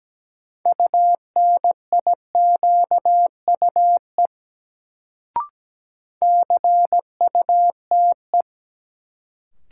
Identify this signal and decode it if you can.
Morse code, heard as a keyed tone, about 17 wpm, 700 Hz. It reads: UNIQUE CUTE
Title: Sentences from Top 1000 Words - Review Part 7 - New Word Drill - 17wpm